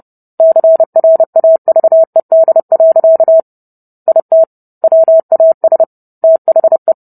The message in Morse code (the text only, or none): CRAVED. IT WAS THE